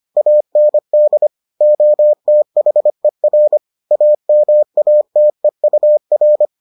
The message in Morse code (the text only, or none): AND OTHER AMATEUR